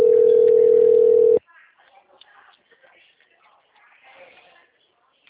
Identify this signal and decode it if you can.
Ringback tone (North America), 440 + 480 Hz, with speech.
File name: out- -2007- - - .15450